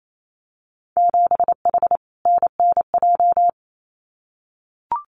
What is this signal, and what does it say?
Morse code, heard as a keyed tone, about 28 wpm, 700 Hz. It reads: 7H NNJ
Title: Winter Field Day Exchanges Encore 28wpm